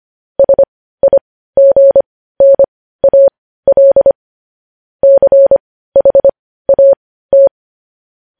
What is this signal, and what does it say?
Morse code, heard as a keyed tone, about 25 wpm, spaced as for 15 wpm, 550 Hz. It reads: SIGNAL CHAT